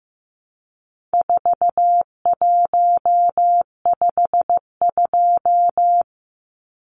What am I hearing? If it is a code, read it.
Morse code, heard as a keyed tone, about 15 wpm, 700 Hz. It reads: 4152
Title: Numbers - 4-Digit 15wpm